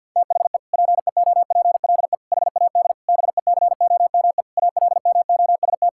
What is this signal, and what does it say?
Morse code, heard as a keyed tone, about 50 wpm, 700 Hz. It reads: THE PEOPLE HAD BECOME ALMOST
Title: Sentences from Top 1000 Words - Review Part 6 - 50wpm